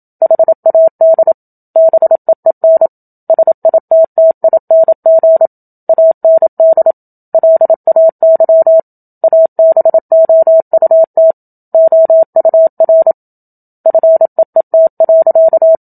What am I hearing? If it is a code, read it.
Morse code, heard as a keyed tone, about 27 wpm, 650 Hz. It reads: HAD BEEN SITTING AND LAY ABOUT OUR FEET.
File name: Book Two / 01-Chapter 1 - 27 WPM